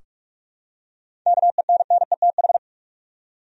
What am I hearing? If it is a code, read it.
Morse code, heard as a keyed tone, about 46 wpm, 700 Hz. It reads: KENNETH
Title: USA Common Names - 45wpm